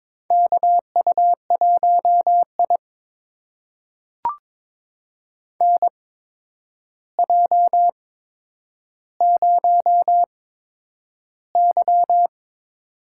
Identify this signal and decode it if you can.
Morse code, heard as a keyed tone, about 22 wpm, spaced as for 9 wpm, 700 Hz. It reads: KU1I NJ0Y
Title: USA Call Signs - 4 Characters - 8x Character Spacing - 22wpm